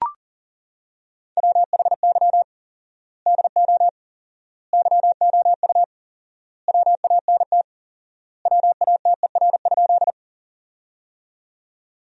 Morse code, heard as a keyed tone, about 40 wpm, 700 Hz. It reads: WHY DO YOU WANT WATER?
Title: 4x Word-Spacing - Sentences from Top 200 Words - 40wpm